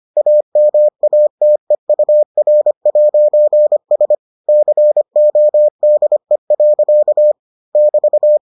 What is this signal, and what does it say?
Morse code, heard as a keyed tone, about 25 wpm, 600 Hz. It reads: AMATEUR'S CODE. <BT>